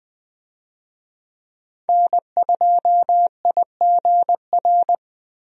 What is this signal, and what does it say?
Morse code, heard as a keyed tone, about 20 wpm, 700 Hz. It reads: N2IGR
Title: USA Call Signs - 5 Characters - Repeated 2x - 20wpm